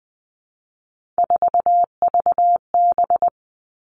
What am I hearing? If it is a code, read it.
Morse code, heard as a keyed tone, about 20 wpm, 700 Hz. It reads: 4VB
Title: Three Challenging Letter-Number Combinations 20wpm